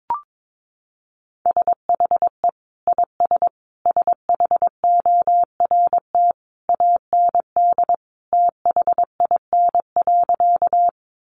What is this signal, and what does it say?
Morse code, heard as a keyed tone, about 22 wpm, 700 Hz. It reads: SHE IS SHORT AND THIN.
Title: Sentences from Top 1200 Words 22wpm